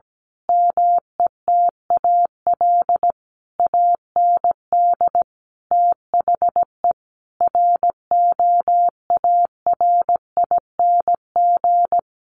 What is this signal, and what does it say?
Morse code, heard as a keyed tone, about 17 wpm, 700 Hz. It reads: METAL AND THE ROARING
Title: A Sound Of Thunder 17wpm